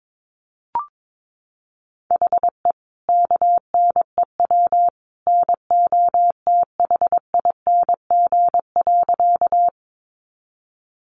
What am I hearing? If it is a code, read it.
Morse code, heard as a keyed tone, about 22 wpm, 700 Hz. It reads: HE KNEW NOTHING.
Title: Sentences from Top 500 Words 22wpm